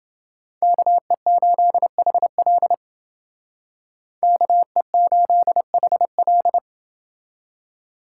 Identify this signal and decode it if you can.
Morse code, heard as a keyed tone, about 29 wpm, 700 Hz. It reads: KE8HL KE8HL